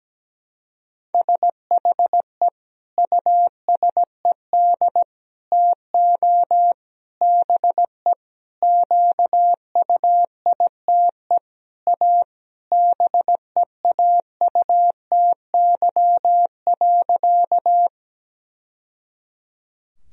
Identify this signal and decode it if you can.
Morse code, heard as a keyed tone, about 17 wpm, 700 Hz. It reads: SHE USED TO BE QUITE A BEAUTY.
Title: Sentences from Top 1000 Words - Review Part 2 - 17wpm